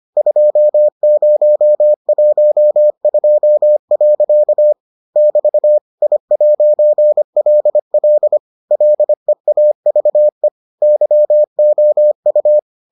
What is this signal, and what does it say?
Morse code, heard as a keyed tone, about 25 wpm, 600 Hz. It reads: 2012. <BT> I'LL LEAVE YOU